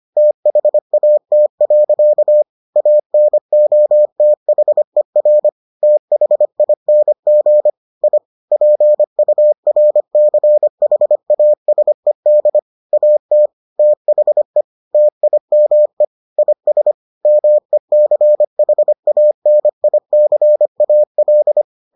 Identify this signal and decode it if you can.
Morse code, heard as a keyed tone, about 25 wpm, 600 Hz. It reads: THAT. ANOTHER THING I PURCHASED AT THE TIME IS MECHANICAL